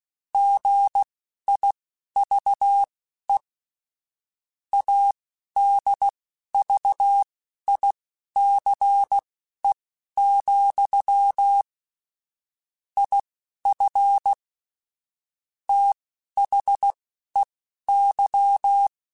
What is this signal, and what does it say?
Morse code, heard as a keyed tone, about 16 wpm, spaced as for 11 wpm, 800 Hz. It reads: GIVE ADVICE, IF THEY